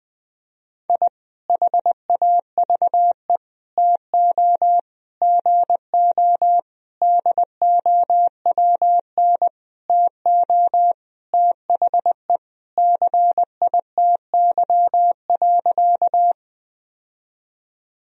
Morse code, heard as a keyed tone, about 20 wpm, 700 Hz. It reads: I HAVE TO GO DOWN TO THE CITY.
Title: Sentences from Top 1000 Words - Review Part 1 - 20wpm